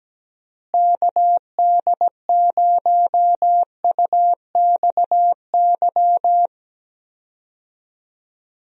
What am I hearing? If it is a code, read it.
Morse code, heard as a keyed tone, about 17 wpm, 700 Hz. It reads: KD0UXY
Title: USA Call Signs - 6 Characters - Repeated 3x - 17wpm